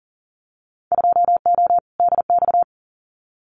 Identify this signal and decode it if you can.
Morse code, heard as a keyed tone, about 38 wpm, 700 Hz. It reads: 2O DX
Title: Winter Field Day Exchanges Encore 40wpm